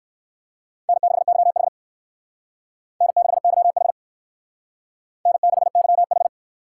Morse code, heard as a keyed tone, about 56 wpm, 700 Hz. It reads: N6YH N6YH N6YH